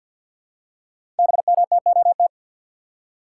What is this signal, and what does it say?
Morse code, heard as a keyed tone, about 50 wpm, 700 Hz. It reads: BMTOT
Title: Code Groups - 5 Letters - 50wpm